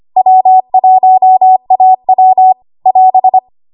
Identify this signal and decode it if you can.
Morse code, heard as a keyed tone, about 25 wpm, 750 Hz. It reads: W1AW <AS>
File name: Archive / 25 WPM / 25WPM